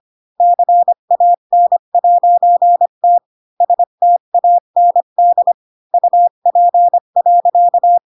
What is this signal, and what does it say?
Morse code, heard as a keyed tone, about 25 wpm, 700 Hz. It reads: CAN'T STAND UP.
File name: Pinocchio / Pinocchio ch7 20 25 ik1hge